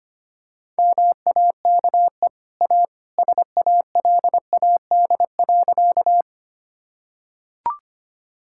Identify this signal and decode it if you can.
Morse code, heard as a keyed tone, about 25 wpm, 700 Hz. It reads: MAKE A SALAD.
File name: Easy Sentences with Hints - 25wpm